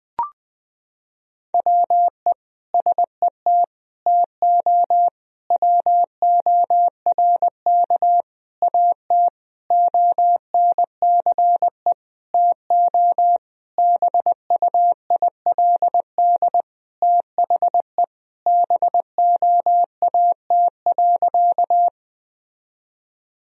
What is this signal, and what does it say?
Morse code, heard as a keyed tone, about 20 wpm, 700 Hz. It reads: WE SET TO WORK AT ONCE TO BUILD THE BOAT.
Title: Sentences from Top 1000 Words - Review Part 1 - 20wpm